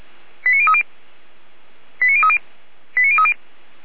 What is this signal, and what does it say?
Five-tone selective call (ZVEI), heard as quick runs of five tones, tones of about 70 ms. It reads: 89030 89030 89030